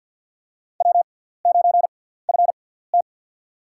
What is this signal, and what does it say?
Morse code, heard as a keyed tone, about 47 wpm, spaced as for 17 wpm, 700 Hz. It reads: W9FT
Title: USA Call Signs - 4 Characters - 6x Character Spacing - 50wpm